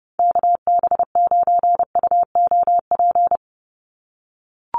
Morse code, heard as a keyed tone, about 30 wpm, 700 Hz. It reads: KB9UOP
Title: USA Call Signs 30wpm